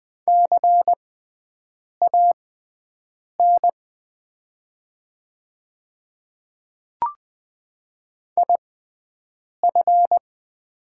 Morse code, heard as a keyed tone, about 20 wpm, spaced as for 7 wpm, 700 Hz. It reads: CAN IF
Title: 6x Character Spacing - Top 100 Words - 20wpm